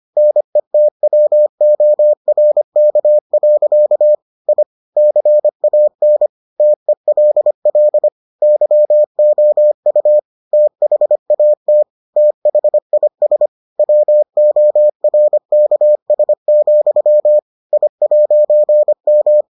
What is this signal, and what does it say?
Morse code, heard as a keyed tone, about 25 wpm, 600 Hz. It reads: NETWORK. I CAN TELL YOU THAT THIS WORKS, I'M